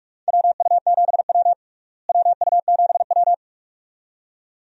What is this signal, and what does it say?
Morse code, heard as a keyed tone, about 46 wpm, 700 Hz. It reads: WU7W WU7W
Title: USA Call Signs - 4 Characters - Repeated 2x - 45wpm